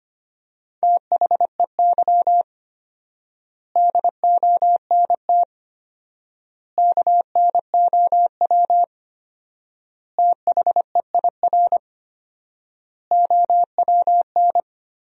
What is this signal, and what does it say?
Morse code, heard as a keyed tone, about 25 wpm, 700 Hz. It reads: THEY DONT KNOW THEIR OWN
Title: 4x Word-Spacing - Sentences from Top 300 Words - 25wpm